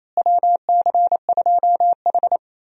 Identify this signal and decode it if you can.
Morse code, heard as a keyed tone, about 28 wpm, 700 Hz. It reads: WC2H
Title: USA Call Signs - 4 Characters - Repeated 2x - 28wpm